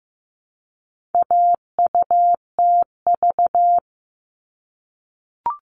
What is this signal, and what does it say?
Morse code, heard as a keyed tone, about 15 wpm, 700 Hz. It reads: AUTV
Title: All Cut Numbers - 4-Digit 15wpm